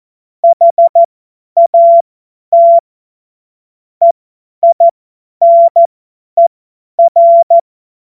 Morse code, heard as a keyed tone, about 14 wpm, spaced as for 10 wpm, 680 Hz. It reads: HAT EINER
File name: Morsekurs / Witze / Witz 05. Oktober 2021, 50 F70